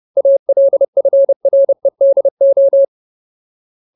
Morse code, heard as a keyed tone, about 30 wpm, 550 Hz. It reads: ALFREDO
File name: Ultimate Word List / 30WPM Ultimate